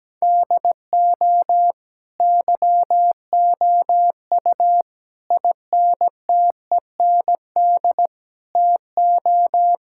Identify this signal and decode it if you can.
Morse code, heard as a keyed tone, about 17 wpm, 700 Hz. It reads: DO YOU INTEND TO
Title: Sentences from Top 1600 Words 17wpm